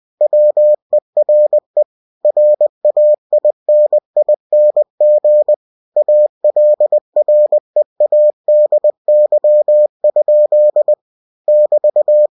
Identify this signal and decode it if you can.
Morse code, heard as a keyed tone, about 20 wpm, 600 Hz. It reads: WERE RAINING ALREADY? <BT>
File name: cw-books / winnie-the-pooh / winnie-20wpm-page-0035